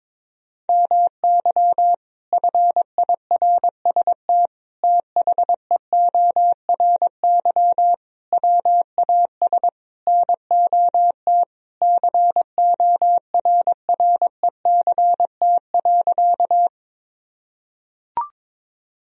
Morse code, heard as a keyed tone, about 21 wpm, 700 Hz. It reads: MY FIRST THEORY WAS NOT CORRECT.
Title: Sentences from Top 1000 Words - Review Part 3 - 22wpm